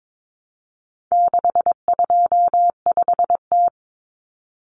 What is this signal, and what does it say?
Morse code, heard as a keyed tone, about 22 wpm, 700 Hz. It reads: 625T
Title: Common Cut Numbers - 4-Digit 22wpm